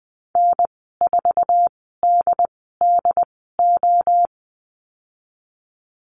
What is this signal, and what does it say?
Morse code, heard as a keyed tone, about 20 wpm, spaced as for 15 wpm, 700 Hz. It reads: N4DDO
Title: 088 Lesson 15g - Number 5 - Call Signs 20wpm